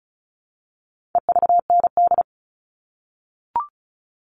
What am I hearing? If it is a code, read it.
Morse code, heard as a keyed tone, about 34 wpm, 700 Hz. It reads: EVND
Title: All Cut Numbers - 4-Digit 35wpm